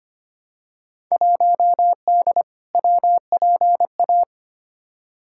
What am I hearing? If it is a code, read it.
Morse code, heard as a keyed tone, about 25 wpm, 700 Hz. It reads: 1D WPA
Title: Field Day Exchanges - 25wpm